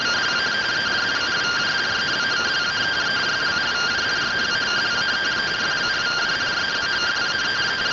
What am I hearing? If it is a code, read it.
Two-tone teleprinter signal, about 170 Hz shift.
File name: findings / MR AXIAL 71mm 3mm.gen-11 rtty rx